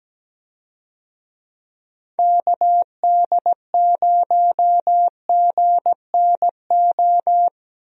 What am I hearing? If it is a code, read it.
Morse code, heard as a keyed tone, about 17 wpm, 700 Hz. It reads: KD0GNO
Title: USA Call Signs - 6 Characters - Repeated 2x - 17wpm